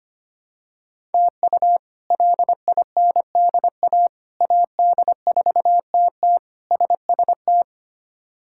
Morse code, heard as a keyed tone, about 25 wpm, 700 Hz. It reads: TU LINDA AD4TT SST
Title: K1USN SST QSO Practice - 25wpm